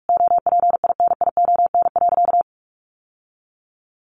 Morse code, heard as a keyed tone, about 44 wpm, 700 Hz. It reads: OPINION.
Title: Sentences from Top 1200 Words 45wpm